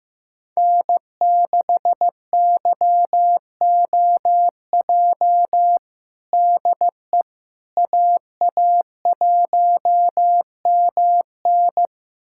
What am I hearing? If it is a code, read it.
Morse code, heard as a keyed tone, about 15 wpm, 700 Hz. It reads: N6YOJ DE AA1MN
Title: USA Callsign DE Callsign - 5 Characters - 15wpm